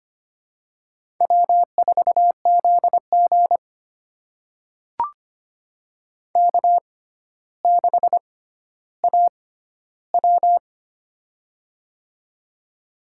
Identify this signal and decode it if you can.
Morse code, heard as a keyed tone, about 25 wpm, spaced as for 13 wpm, 700 Hz. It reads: W4ZG K6AW
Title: USA Call Signs - 4 Characters - 6x Character Spacing - 25wpm